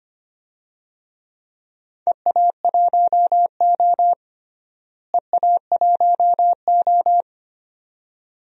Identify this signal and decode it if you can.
Morse code, heard as a keyed tone, about 25 wpm, 700 Hz. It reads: EA1O EA1O
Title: DXCC-100 Call Signs - 4 Characters - Repeated 2x - 25wpm